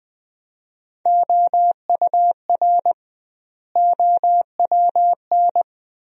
Morse code, heard as a keyed tone, about 21 wpm, 700 Hz. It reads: OUR OWN
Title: 2x Word-Spacing - Sentences from Top 300 Words - 20wpm